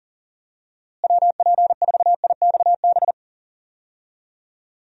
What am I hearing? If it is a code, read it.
Morse code, heard as a keyed tone, about 40 wpm, 700 Hz. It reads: WP4IXB
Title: USA Call Signs - 6 Characters - Repeated 2x - 40wpm